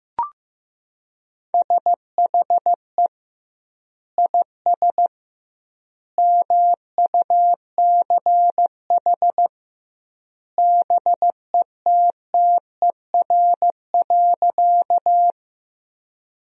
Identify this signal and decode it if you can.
Morse code, heard as a keyed tone, about 15 wpm, 700 Hz. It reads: SHE IS MUCH BETTER.
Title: 2x Word-Spacing - Sentences from Top 400 Words - 15wpm